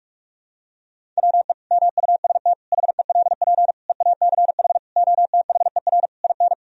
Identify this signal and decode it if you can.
Morse code, heard as a keyed tone, about 46 wpm, 700 Hz. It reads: WE MUST HELP EACH OTHER IN
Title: Sentences from Top 1000 Words - Review Part 1 - 45wpm